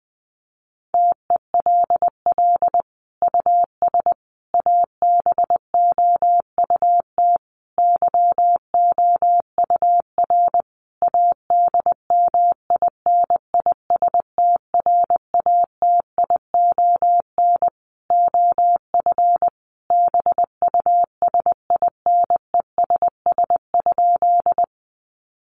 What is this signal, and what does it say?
Morse code, heard as a keyed tone, about 19 wpm, 700 Hz. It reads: TELL US ABOUT YOUR ADMINISTRATION OF BUSINESS?
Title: Sentences from Top 1300 Words 20wpm